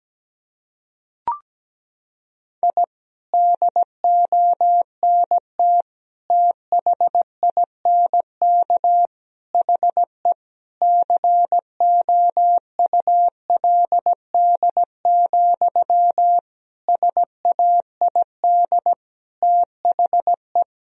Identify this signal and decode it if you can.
Morse code, heard as a keyed tone, about 17 wpm, 700 Hz. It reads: I DONT THINK HE COULD, SAID THE
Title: Sentences from Top 1000 Words - Review Part 4 - 17wpm